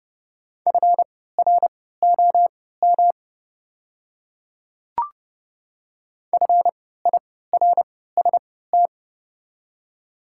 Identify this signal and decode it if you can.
Morse code, heard as a keyed tone, about 30 wpm, spaced as for 17 wpm, 700 Hz. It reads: FROM FIRST
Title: 3x Character Spacing - Top 100 Words - 30wpm